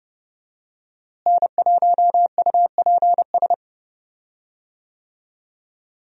Morse code, heard as a keyed tone, about 30 wpm, 700 Hz. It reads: N1UPS
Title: USA Call Signs - 5 Characters - Repeated 3x - 30wpm